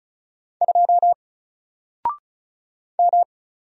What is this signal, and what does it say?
Morse code, heard as a keyed tone, about 35 wpm, 700 Hz. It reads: 2 M
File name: Single-Letter-Number-35wpm